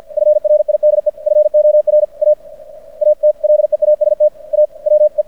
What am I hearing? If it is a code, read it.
Morse code, heard as a keyed tone, about 50 wpm, 600 Hz. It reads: FRIDE FOR A AT LEAST A WE
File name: archived / CRSnet